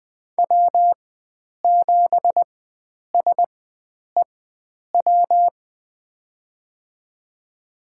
Morse code, heard as a keyed tone, about 20 wpm, spaced as for 9 wpm, 700 Hz. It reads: W7SEW
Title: USA Call Signs - 5 Characters - 4x Character Spacing - 20wpm